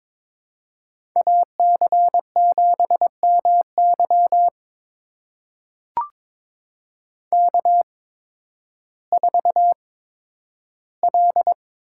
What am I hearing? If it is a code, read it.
Morse code, heard as a keyed tone, about 22 wpm, spaced as for 12 wpm, 700 Hz. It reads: AC7MY K4L
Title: USA Call Signs - 5 Characters - 8x Character Spacing - 22wpm